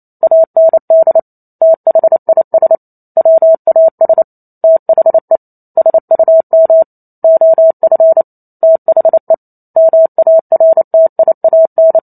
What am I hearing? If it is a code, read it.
Morse code, heard as a keyed tone, about 28 wpm, 650 Hz. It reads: AND THIS WAS THE SUM OF THE MARTIAN